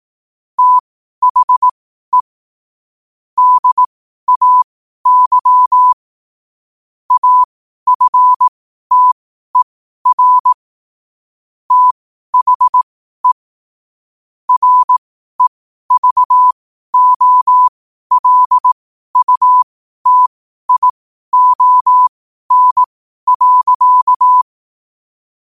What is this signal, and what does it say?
Morse code, heard as a keyed tone, about 18 wpm, spaced as for 12 wpm, 1000 Hz. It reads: THE DAY AFTER THE REVOLUTION.
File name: qotdfiles / 12wpm